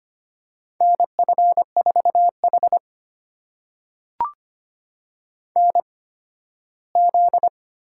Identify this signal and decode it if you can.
Morse code, heard as a keyed tone, about 25 wpm, spaced as for 13 wpm, 700 Hz. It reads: N F4H NZ
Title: USA Call Signs - 4 Characters - 8x Character Spacing - 25wpm